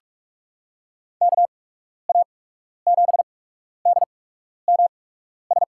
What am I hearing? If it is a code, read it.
Morse code, heard as a keyed tone, about 47 wpm, spaced as for 12 wpm, 700 Hz. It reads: KA7DMS
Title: USA Call Signs - 8x Character Spacing - 45wpm